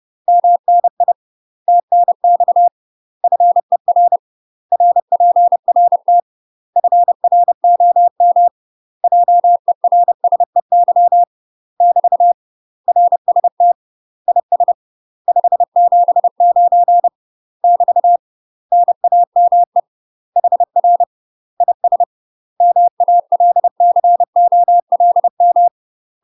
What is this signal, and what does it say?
Morse code, heard as a keyed tone, about 30 wpm, 700 Hz. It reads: MNI TNX FER RPRT FROM JERSEY <BT> RST IS 579 <BT> NAME HR IS MALCOLM
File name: UK MASTER QSOs TRAINING / 30WPM / UK MASTER QSOs 0050